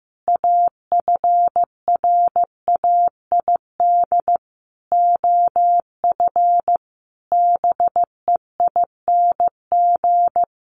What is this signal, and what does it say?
Morse code, heard as a keyed tone, about 15 wpm, 700 Hz. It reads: AFRAID OF BEING
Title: Sentences from Top 1500 Words 15wpm